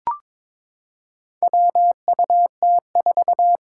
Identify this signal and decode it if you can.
Morse code, heard as a keyed tone, about 22 wpm, 700 Hz. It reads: WUT4
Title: Four Challenging Letter-Number Combinations 22wpm